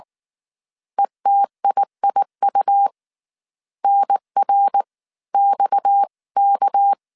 Morse code, heard as a keyed tone, about 19 wpm, 800 Hz. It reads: ETIIU NR XK